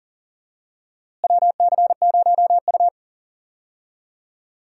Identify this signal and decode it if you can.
Morse code, heard as a keyed tone, about 40 wpm, 700 Hz. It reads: WC0U
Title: USA Call Signs - 4 Characters - Repeated 2x - 40wpm